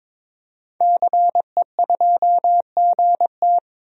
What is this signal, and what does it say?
Morse code, heard as a keyed tone, about 21 wpm, 700 Hz. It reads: CE2GT
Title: DXCC-100 Call Signs - 5 Characters - 22wpm